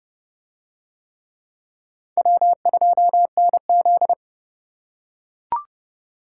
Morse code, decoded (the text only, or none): W2NZ